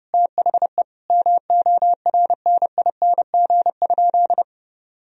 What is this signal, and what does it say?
Morse code, heard as a keyed tone, about 32 wpm, 700 Hz. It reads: THE MORNING?
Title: Sentences from Top 1000 Words - Review Part 2 - 30wpm